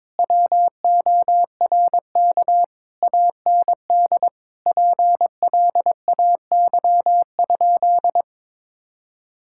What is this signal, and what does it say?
Morse code, heard as a keyed tone, about 22 wpm, 700 Hz. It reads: WORK AND PLAY?
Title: Sentences from Top 1000 Words - Review Part 6 - 22wpm